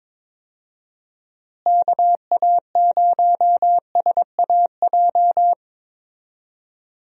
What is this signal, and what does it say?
Morse code, heard as a keyed tone, about 21 wpm, 700 Hz. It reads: KA0SAJ
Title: USA Call Signs - 6 Characters - Repeated 3x - 22wpm